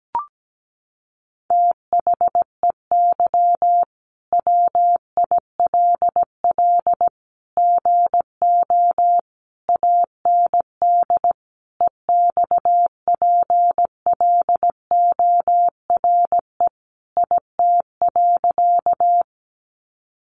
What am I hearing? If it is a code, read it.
Morse code, heard as a keyed tone, about 17 wpm, 700 Hz. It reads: THEY WILL GO AND EXPLORE IT.
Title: Sentences from Top 1100 Words 17wpm